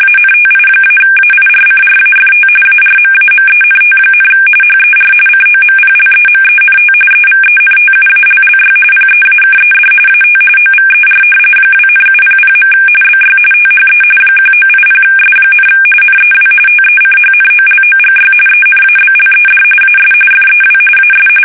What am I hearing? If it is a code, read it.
RTTY, about 45.45 baud. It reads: ANTI-AIRCRAFT ARMAMENT, HER COMPLEMENT EVENTUALLY PEAKED AT OVER 2,700. BY THE 1980S, NEW TECHNOLOGY ENABLED AS FEW AS
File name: fsk850 / battleshipnj